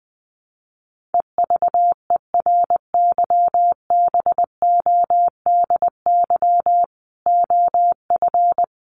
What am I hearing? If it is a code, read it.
Morse code, heard as a keyed tone, about 19 wpm, 700 Hz. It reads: EVERYBODY OF
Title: Sentences from Top 1400 Words 20wpm